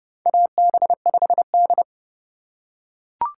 Morse code, heard as a keyed tone, about 30 wpm, 700 Hz. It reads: AB5D
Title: USA Call Signs - 4 Characters - 2x Character Spacing - 30wpm